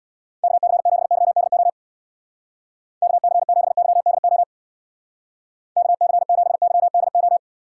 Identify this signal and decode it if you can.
Morse code, heard as a keyed tone, about 62 wpm, 700 Hz. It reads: KC7YDO KC7YDO KC7YDO
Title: Speed Racing - USA 2x3 Call Signs 50wpm